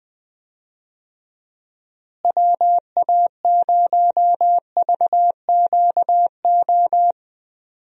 Morse code, heard as a keyed tone, about 20 wpm, 700 Hz. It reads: WA0VQO
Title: USA Call Signs - 6 Characters - Repeated 3x - 20wpm